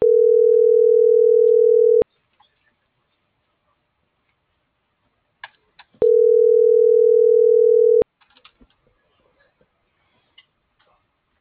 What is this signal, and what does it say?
Ringback tone (North America), 440 + 480 Hz.